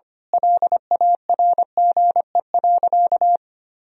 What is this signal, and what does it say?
Morse code, heard as a keyed tone, about 25 wpm, 700 Hz. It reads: LARGE.
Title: Sentences from Top 1000 Words - Review Part 1 - 25wpm